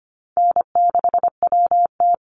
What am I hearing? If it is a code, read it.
Morse code, heard as a keyed tone, about 25 wpm, 700 Hz. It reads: N6WT
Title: NA SOTA Chaser Callsigns 25wpm